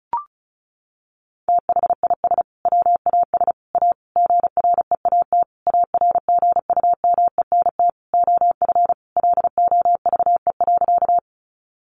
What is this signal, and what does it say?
Morse code, heard as a keyed tone, about 34 wpm, 700 Hz. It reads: THIS WAS A GREAT ARGUMENT OF LOVE.
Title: Sentences from Top 1400 Words 35wpm